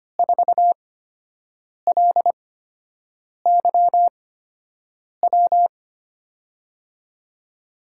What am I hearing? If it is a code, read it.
Morse code, heard as a keyed tone, about 25 wpm, spaced as for 7 wpm, 700 Hz. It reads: 4LYW
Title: USA Call Signs - 8x Character Spacing - 25wpm